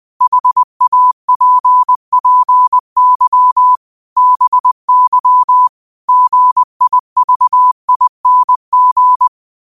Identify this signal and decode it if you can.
Morse code, heard as a keyed tone, about 20 wpm, 1000 Hz. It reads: HAPPY BY GIVING
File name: qotdfiles / 20wpm